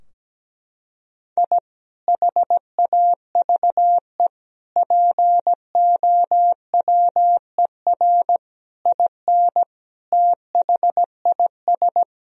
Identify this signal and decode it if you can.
Morse code, heard as a keyed tone, about 17 wpm, 700 Hz. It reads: I HAVE POWER IN THIS
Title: Sentences from Top 1000 Words - Review Part 2 - 17wpm